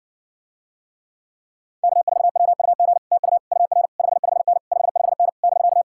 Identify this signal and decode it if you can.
Morse code, heard as a keyed tone, about 60 wpm, 700 Hz. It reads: K4WUG TU UR 55N 55N <BK>